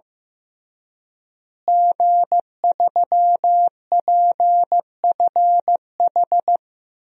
Morse code, heard as a keyed tone, about 15 wpm, 700 Hz. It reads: G3PFH
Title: DXCC-100 Call Signs - 5 Characters - Repeated 3x - 15wpm